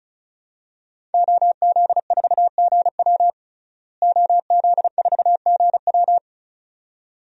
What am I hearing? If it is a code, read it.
Morse code, heard as a keyed tone, about 34 wpm, 700 Hz. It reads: OZ4GW OZ4GW
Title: DXCC-100 Call Signs - 5 Characters - Repeated 2x - 35wpm